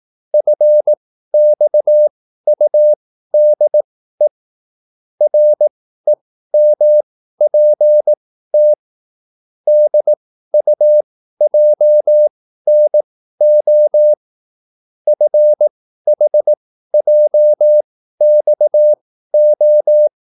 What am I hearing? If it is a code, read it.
Morse code, heard as a keyed tone, about 18 wpm, spaced as for 13 wpm, 600 Hz. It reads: FXUDE REMPT DUJNO FHJXO